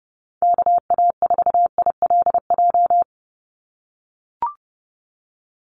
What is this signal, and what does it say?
Morse code, heard as a keyed tone, about 30 wpm, 700 Hz. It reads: KA4ILJ
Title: USA Call Signs 30wpm